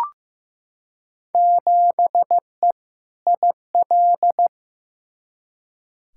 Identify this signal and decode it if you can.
Morse code, heard as a keyed tone, about 15 wpm, 700 Hz. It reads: 7E IL
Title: Field Day Exchanges Encore - 15wpm